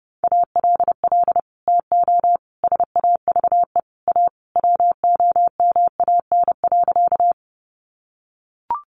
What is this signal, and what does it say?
Morse code, heard as a keyed tone, about 30 wpm, 700 Hz. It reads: ALL TO SAVE A WOMAN.
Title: Sentences from Top 1100 Words 30wpm